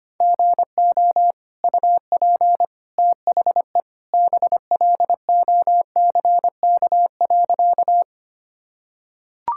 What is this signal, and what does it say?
Morse code, heard as a keyed tone, about 25 wpm, 700 Hz. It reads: GO UP THE BLOCK.
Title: Sentences from Top 1600 Words 25wpm